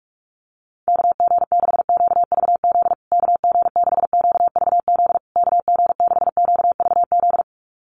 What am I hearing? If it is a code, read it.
Morse code, heard as a keyed tone, about 44 wpm, 700 Hz. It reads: KG6QVZ KG6QVZ KG6QVZ
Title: USA Call Signs Repeated 3x 45wpm